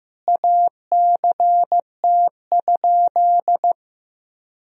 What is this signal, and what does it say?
Morse code, heard as a keyed tone, about 15 wpm, 700 Hz. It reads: ACT?
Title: Sentences from Top 1000 Words - Review Part 2 - 15wpm